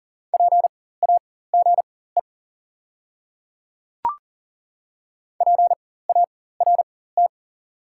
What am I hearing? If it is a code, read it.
Morse code, heard as a keyed tone, about 37 wpm, spaced as for 18 wpm, 700 Hz. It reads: PAGE PART
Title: 4x Character Spacing - Top 300 Words - 40wpm